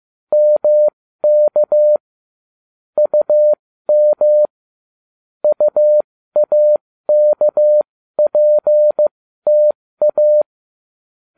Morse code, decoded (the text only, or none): MK UM UAKPTA